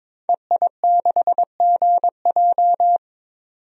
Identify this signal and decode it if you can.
Morse code, heard as a keyed tone, about 22 wpm, 700 Hz. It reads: EI6GJ
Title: DXCC-100 Call Signs - 5 Characters - 22wpm